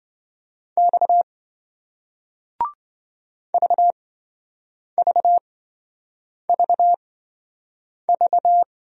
Morse code, heard as a keyed tone, about 27 wpm, spaced as for 7 wpm, 700 Hz. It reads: X VVVV